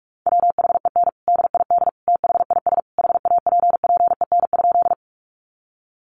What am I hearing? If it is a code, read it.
Morse code, heard as a keyed tone, about 44 wpm, 700 Hz. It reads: WHEN DID THIS HAPPEN?
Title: Sentences from Top 400 Words 45wpm